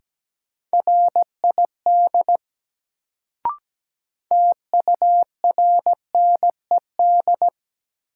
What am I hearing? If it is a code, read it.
Morse code, heard as a keyed tone, about 17 wpm, 700 Hz. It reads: RID TURNED
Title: Sets of 4 Words - New Word Drill - 17wpm